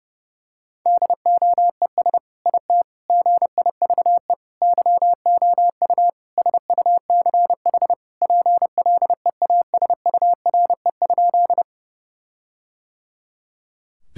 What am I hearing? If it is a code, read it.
Morse code, heard as a keyed tone, about 32 wpm, 700 Hz. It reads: DOES IT GIVE YOU SUCH PLEASURE?
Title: Sentences from Top 1600 Words 30wpm